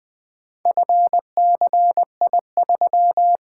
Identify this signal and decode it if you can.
Morse code, heard as a keyed tone, about 24 wpm, 700 Hz. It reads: FCI3
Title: Speed-Racing - Four Letter-Number Combinations - 20wpm